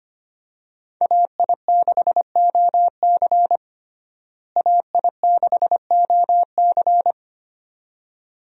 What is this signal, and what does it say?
Morse code, heard as a keyed tone, about 25 wpm, 700 Hz. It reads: AI6OC AI6OC
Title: USA Call Signs - 5 Characters - Repeated 2x - 25wpm